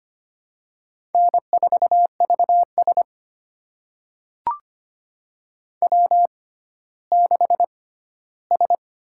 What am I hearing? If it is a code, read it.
Morse code, heard as a keyed tone, about 25 wpm, spaced as for 14 wpm, 700 Hz. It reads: N4VS W6S
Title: USA Call Signs - 4 Characters - 6x Character Spacing - 25wpm